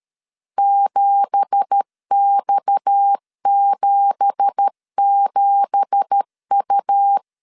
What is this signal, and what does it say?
Morse code, heard as a keyed tone, about 13 wpm, 800 Hz. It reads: 7X77U
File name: LEZIONI / 9 2867 12wpm 2322